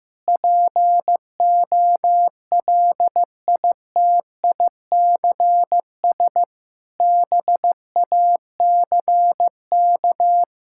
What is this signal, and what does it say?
Morse code, heard as a keyed tone, about 15 wpm, 700 Hz. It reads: POLITICS BACK
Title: Sentences from Top 1000 Words - Review Part 6 - 15wpm